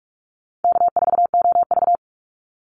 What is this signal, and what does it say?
Morse code, heard as a keyed tone, about 44 wpm, 700 Hz. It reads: K4OV
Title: Easy Call Signs - 45wpm